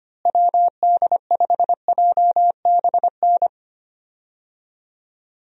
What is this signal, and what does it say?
Morse code, heard as a keyed tone, about 25 wpm, 700 Hz. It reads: WD5JBN
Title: USA Call Signs - 6 Characters - Repeated 2x - 25wpm